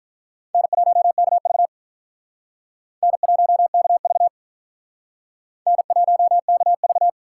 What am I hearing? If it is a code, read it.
Morse code, heard as a keyed tone, about 45 wpm, 700 Hz. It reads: N1KV N1KV N1KV